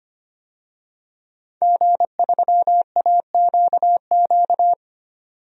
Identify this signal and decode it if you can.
Morse code, heard as a keyed tone, about 25 wpm, 700 Hz. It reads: G3AQQ
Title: DXCC-100 Call Signs - 5 Characters - 25wpm